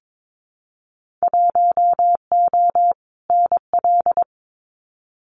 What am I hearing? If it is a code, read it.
Morse code, heard as a keyed tone, about 22 wpm, 700 Hz. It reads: 1O NL
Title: Winter Field Day Exchanges Encore 22wpm